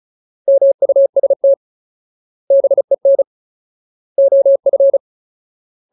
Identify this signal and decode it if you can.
Morse code, heard as a keyed tone, about 34 wpm, 550 Hz. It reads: MUST BEN OF
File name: Ultimate Word List / 35WPM Ultimate